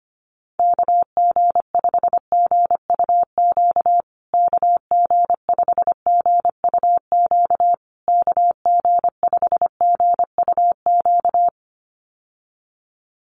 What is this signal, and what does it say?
Morse code, heard as a keyed tone, about 25 wpm, 700 Hz. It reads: KG5GUQ KG5GUQ KG5GUQ
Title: USA Call Signs Repeated 3x 25wpm